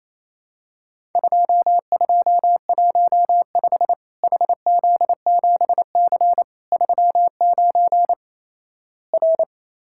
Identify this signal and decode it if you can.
Morse code, heard as a keyed tone, about 28 wpm, 700 Hz. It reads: 2215 HZ7C 39 R
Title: Worked All Europe DX Contest - QTC Exchange - 28wpm